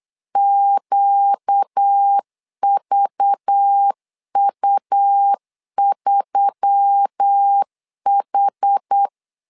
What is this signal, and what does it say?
Morse code, heard as a keyed tone, about 8 wpm, 800 Hz. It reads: QVU3H
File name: LEZIONI / 5 LBVC / 5 LBV 8wpm 2837